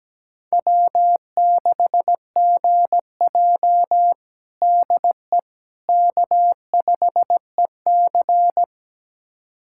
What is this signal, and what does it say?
Morse code, heard as a keyed tone, about 17 wpm, 700 Hz. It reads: W6GJ DE K5EC
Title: USA Callsign DE Callsign - 4 Characters - 17wpm